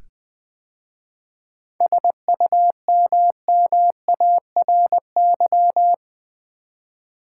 Morse code, heard as a keyed tone, about 20 wpm, 700 Hz. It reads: SUMMARY
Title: 7 Letter Words - First 3 Letters Given - 20wpm